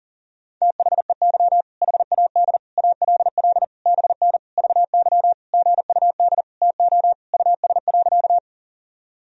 Morse code, heard as a keyed tone, about 40 wpm, 700 Hz. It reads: THEY HAD ALL BN VY GUD TO US.
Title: Sentences from Top 200 Words and Abbreviations 40wpm